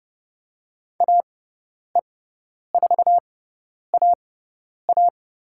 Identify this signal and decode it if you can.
Morse code, heard as a keyed tone, about 30 wpm, spaced as for 10 wpm, 700 Hz. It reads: AE4AA
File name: Lesson 18g - Letter m - Call Signs 30wpm